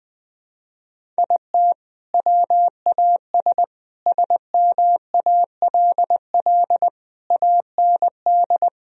Morse code, heard as a keyed tone, about 20 wpm, 700 Hz. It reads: IT WAS SMALL AND
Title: Sentences from Top 1000 Words - Review Part 1 - 20wpm